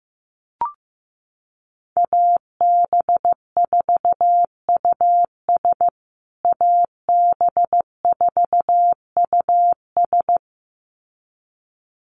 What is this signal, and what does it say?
Morse code, heard as a keyed tone, about 15 wpm, 700 Hz. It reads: AB4US AB4US
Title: USA Call Signs Repeated 2x 15wpm